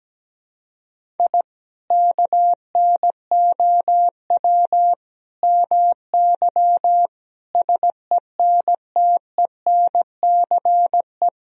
Morse code, heard as a keyed tone, about 17 wpm, 700 Hz. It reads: I KNOW MY SENTENCE
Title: Sentences from Top 1000 Words - Review Part 1 - 17wpm